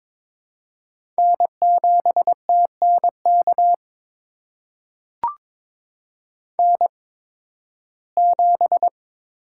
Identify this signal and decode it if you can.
Morse code, heard as a keyed tone, about 22 wpm, spaced as for 14 wpm, 700 Hz. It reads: N7TNK N7
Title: USA Call Signs - 5 Characters - 8x Character Spacing - 22wpm